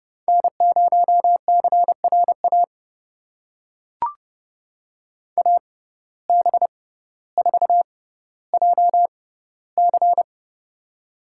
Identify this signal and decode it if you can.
Morse code, heard as a keyed tone, about 31 wpm, spaced as for 15 wpm, 690 Hz. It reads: N0CRA AB4JC
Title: USA Call Signs - 5 Characters - 6x Character Spacing - 30wpm